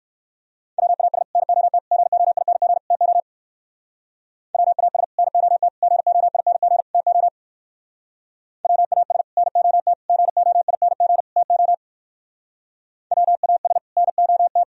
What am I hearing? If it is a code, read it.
Morse code, heard as a keyed tone, about 58 wpm, 700 Hz. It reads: WAS NOT GOING TO WAS NOT GOING TO WAS NOT GOING TO WAS NOT